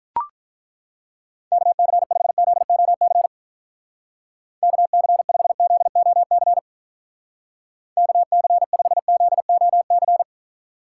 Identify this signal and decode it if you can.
Morse code, heard as a keyed tone, about 43 wpm, 700 Hz. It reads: KC5ZOC KC5ZOC KC5ZOC
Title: Speed Racing - USA Call Signs 35wpm